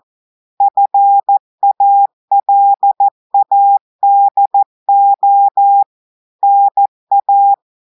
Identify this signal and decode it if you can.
Morse code, heard as a keyed tone, about 14 wpm, 800 Hz. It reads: FALADO NA
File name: audios / portugais / portugais texte en clair 2 vitesse 14 mots minutes